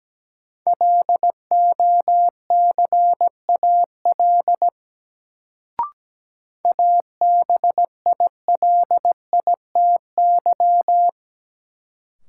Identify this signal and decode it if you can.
Morse code, heard as a keyed tone, about 17 wpm, 700 Hz. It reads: LOCAL ABILITY
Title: Sentences from Top 1000 Words - Review Part 3 - New Word Drill - 17wpm